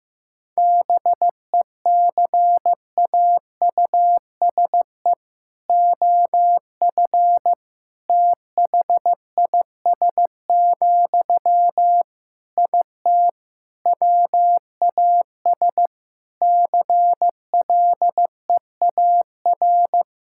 Morse code, heard as a keyed tone, about 15 wpm, 700 Hz. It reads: BECAUSE OF THIS, IT WAS CLEAR
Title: Sentences from Top 1000 Words - Review Part 6 - 15wpm